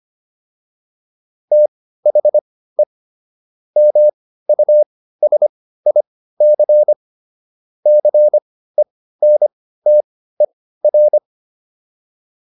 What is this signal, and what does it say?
Morse code, heard as a keyed tone, about 25 wpm, spaced as for 15 wpm, 600 Hz. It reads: THE MUSIC CENTER